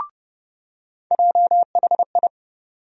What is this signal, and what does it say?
Morse code, heard as a keyed tone, about 30 wpm, 700 Hz. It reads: JHI
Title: USA Call Sign Postfixes - 3 Letters 30wpm